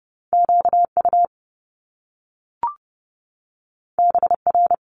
Morse code, heard as a keyed tone, about 30 wpm, 700 Hz. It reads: QU BR
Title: Top Two Letter Combinations in Words 30wpm